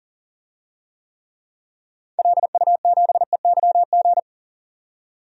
Morse code, heard as a keyed tone, about 39 wpm, 700 Hz. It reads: LU7EYG
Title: DXCC-100 Call Signs - 6 Characters - 40wpm